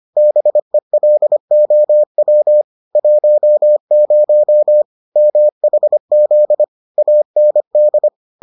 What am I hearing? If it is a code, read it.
Morse code, heard as a keyed tone, about 25 wpm, 600 Hz. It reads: BELOW 10 MHZ AND